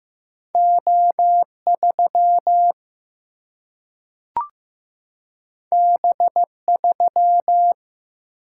Morse code, heard as a keyed tone, about 15 wpm, 700 Hz. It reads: O3 B3
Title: Single-Letter Single-Number 15wpm